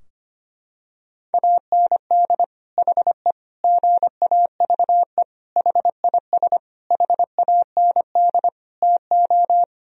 Morse code, heard as a keyed tone, about 25 wpm, 700 Hz. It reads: AND HE GAVE HIS HAND TO
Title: Sentences from Top 1000 Words - Review Part 4 - 25wpm